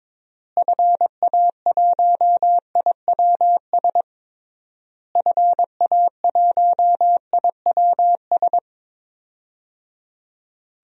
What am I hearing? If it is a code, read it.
Morse code, heard as a keyed tone, about 22 wpm, 700 Hz. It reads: FA1IWS FA1IWS
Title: DXCC-100 Call Signs - 6 Characters - Repeated 3x - 22wpm